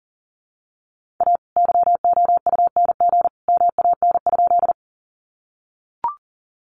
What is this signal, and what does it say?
Morse code, heard as a keyed tone, about 40 wpm, 700 Hz. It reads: A YOUNG MAN?
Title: Sentences from Top 400 Words 40wpm